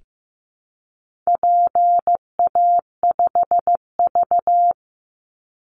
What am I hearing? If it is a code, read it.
Morse code, heard as a keyed tone, about 15 wpm, 700 Hz. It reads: PA5V
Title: Call Signs Repeated 3x - 15wpm